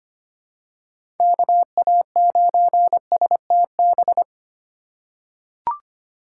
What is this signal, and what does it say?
Morse code, heard as a keyed tone, about 25 wpm, 700 Hz. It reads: KA9STB
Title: USA Call Signs - 6 Characters - 4x Character Spacing - 25wpm